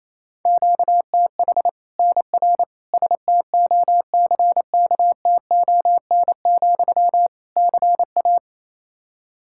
Morse code, heard as a keyed tone, about 28 wpm, 700 Hz. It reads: QTH NR STOCKTON, CA
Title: Practice QSO - Exchange 1 - Beginning - 28wpm